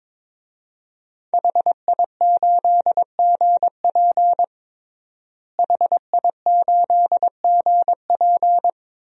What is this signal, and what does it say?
Morse code, heard as a keyed tone, about 22 wpm, 700 Hz. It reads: HI8GP HI8GP
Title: DXCC-100 Call Signs - 5 Characters - Repeated 2x - 22wpm